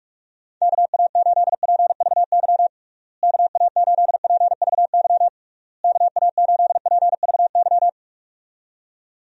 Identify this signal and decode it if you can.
Morse code, heard as a keyed tone, about 41 wpm, 700 Hz. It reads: KA8PVY KA8PVY KA8PVY
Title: USA Call Signs - 6 Characters - Repeated 3x - 45wpm